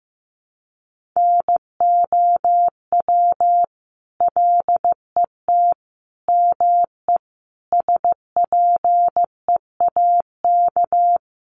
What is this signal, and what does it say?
Morse code, heard as a keyed tone, about 15 wpm, 700 Hz. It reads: NOW LET ME SPEAK
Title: Sentences from Top 1200 Words 15wpm